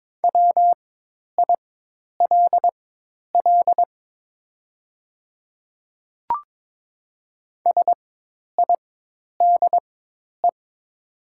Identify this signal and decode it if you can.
Morse code, heard as a keyed tone, about 22 wpm, spaced as for 10 wpm, 700 Hz. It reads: WILL SIDE
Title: 4x Character Spacing - Top 200 Words - 22wpm